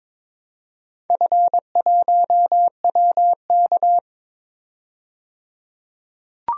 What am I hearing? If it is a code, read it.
Morse code, heard as a keyed tone, about 22 wpm, 700 Hz. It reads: F1WK
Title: DXCC-100 Call Signs - 4 Characters - 22wpm